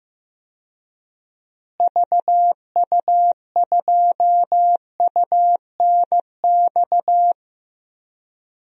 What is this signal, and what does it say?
Morse code, heard as a keyed tone, about 15 wpm, 700 Hz. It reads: VU2UNX
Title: DXCC-100 Call Signs - 6 Characters - Repeated 3x - 15wpm